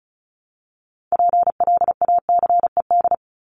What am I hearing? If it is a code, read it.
Morse code, heard as a keyed tone, about 34 wpm, 700 Hz. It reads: PLACED
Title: ED Suffix - Top 100 Words with Base Words 35wpm